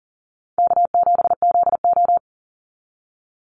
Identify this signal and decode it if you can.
Morse code, heard as a keyed tone, about 40 wpm, 700 Hz. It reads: K7ZO
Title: NA SOTA Chaser Callsigns 40wpm